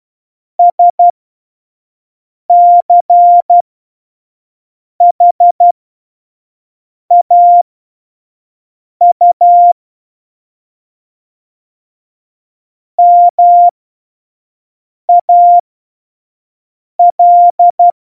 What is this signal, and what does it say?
Morse code, heard as a keyed tone, about 12 wpm, spaced as for 5 wpm, 700 Hz. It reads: SCHAU MAL
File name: Morsekurs / Witze / Witz 03 April 2018, 25 F70